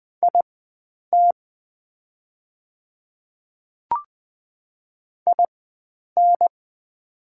Morse code, heard as a keyed tone, about 20 wpm, spaced as for 9 wpm, 690 Hz. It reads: IT IN
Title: 4x Character Spacing - Top 100 Words - 20wpm